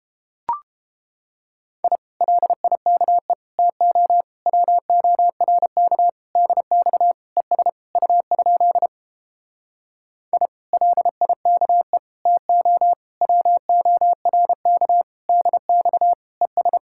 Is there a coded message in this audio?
Yes: Morse code from 1.8 to 16.8 s, heard as a keyed tone, about 34 wpm, 700 Hz. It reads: I LIKE TO WORK DX ES U? I LIKE TO WORK DX ES